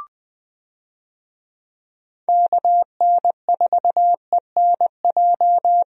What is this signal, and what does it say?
Morse code, heard as a keyed tone, about 20 wpm, 700 Hz. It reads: KN4ENJ
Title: USA Call Signs - 6 Characters - Repeated 2x - 20wpm